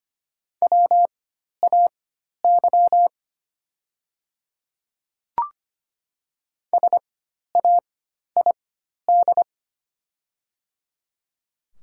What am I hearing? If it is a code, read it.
Morse code, heard as a keyed tone, about 25 wpm, spaced as for 12 wpm, 700 Hz. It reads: WAY SAID
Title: 4x Character Spacing - Top 200 Words - 25wpm